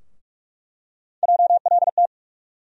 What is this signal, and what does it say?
Morse code, heard as a keyed tone, about 42 wpm, 700 Hz. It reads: JLT
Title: USA Call Sign Postfixes - 3 Letters 45wpm